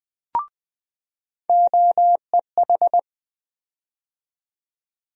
Morse code, heard as a keyed tone, about 20 wpm, 700 Hz. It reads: OEH
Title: USA Call Sign Postfixes - 3 Letters 20wpm